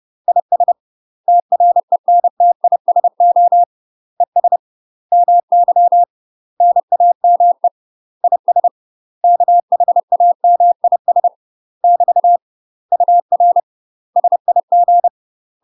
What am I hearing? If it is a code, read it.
Morse code, heard as a keyed tone, about 30 wpm, 700 Hz. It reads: IS TRENTISO ES MY NAME IS KHAMIS <BT> UR SIG
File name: UK MASTER QSOs TRAINING / 30WPM / UK MASTER QSOs 0042